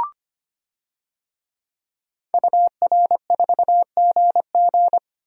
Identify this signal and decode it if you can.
Morse code, heard as a keyed tone, about 25 wpm, 700 Hz. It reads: UR4GG
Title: DXCC-100 Call Signs - 5 Characters - Repeated 3x - 25wpm